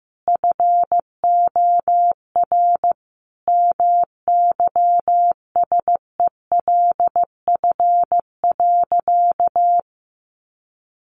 Morse code, heard as a keyed tone, about 15 wpm, 700 Hz. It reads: FOR MYSELF.
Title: Sentences from Top 1300 Words 15wpm